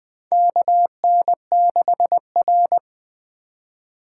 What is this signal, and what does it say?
Morse code, heard as a keyed tone, about 20 wpm, 700 Hz. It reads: KN6R
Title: USA Call Signs - 4 Characters - 3x Character Spacing - 20wpm